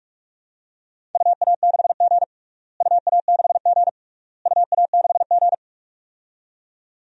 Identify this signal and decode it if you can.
Morse code, heard as a keyed tone, about 46 wpm, 700 Hz. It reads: UA6G UA6G UA6G
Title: DXCC-100 Call Signs - 4 Characters - Repeated 3x - 45wpm